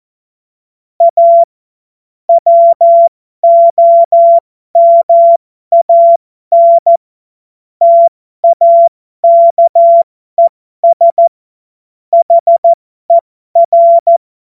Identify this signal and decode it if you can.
Morse code, heard as a keyed tone, about 14 wpm, spaced as for 12 wpm, 680 Hz. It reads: A WOMAN TAKES HER
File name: Morsekurs / Witze